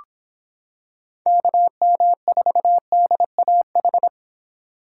Morse code, heard as a keyed tone, about 26 wpm, 700 Hz. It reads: KM4DAH